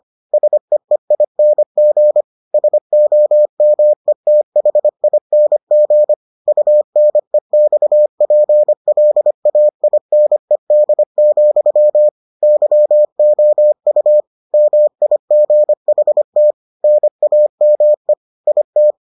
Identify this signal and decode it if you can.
Morse code, heard as a keyed tone, about 25 wpm, 600 Hz. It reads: SEEING SOMETHING UNEXPLAINED, YOU MIGHT NAME IT